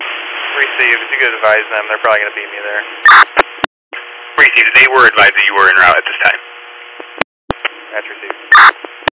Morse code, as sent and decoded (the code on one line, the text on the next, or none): none